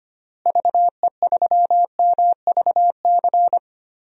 Morse code, heard as a keyed tone, about 25 wpm, 700 Hz. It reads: VE3MVC
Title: DXCC-100 Call Signs - 6 Characters - Repeated 3x - 25wpm